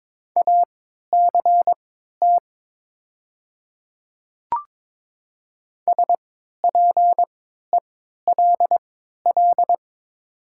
Morse code, heard as a keyed tone, about 22 wpm, spaced as for 13 wpm, 700 Hz. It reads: ACT SPELL
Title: 3x Character Spacing - Top 200 Words - 22wpm